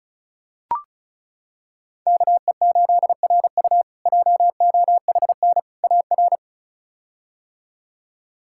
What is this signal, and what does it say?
Morse code, heard as a keyed tone, about 36 wpm, 700 Hz. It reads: KE8RU JOHN AR
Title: K1USN SST QSO Practice - 35wpm